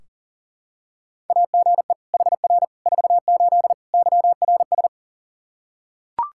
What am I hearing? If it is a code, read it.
Morse code, heard as a keyed tone, about 40 wpm, 700 Hz. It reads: AGE HR 48 YRS
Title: Practice QSO - Exchange 3 - More Info - 40wpm